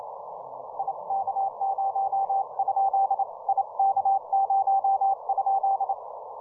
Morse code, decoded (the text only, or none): I K0? I K0?